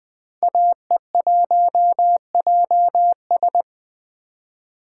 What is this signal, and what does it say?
Morse code, heard as a keyed tone, about 20 wpm, 700 Hz. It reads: AE1JS
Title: 2023 SOTA Chaser Call Signs - 20wpm